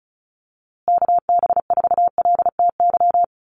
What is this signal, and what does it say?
Morse code, heard as a keyed tone, about 34 wpm, 700 Hz. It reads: KB4LTY
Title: USA Call Signs Repeated 2x 35wpm